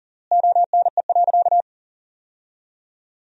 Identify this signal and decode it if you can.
Morse code, heard as a keyed tone, about 37 wpm, 700 Hz. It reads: ONE.
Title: 3x Word-Spacing - Sentences from Top 300 Words - 40wpm